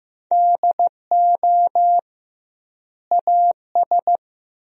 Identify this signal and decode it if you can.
Morse code, heard as a keyed tone, about 15 wpm, 700 Hz. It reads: DO AS
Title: 2x Word-Spacing - Sentences from Top 100 Words - 15wpm